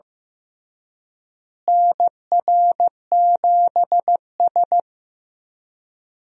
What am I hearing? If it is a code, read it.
Morse code, heard as a keyed tone, about 15 wpm, 700 Hz. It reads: NR7S
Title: USA Call Signs - 4 Characters - Repeated 3x - 15wpm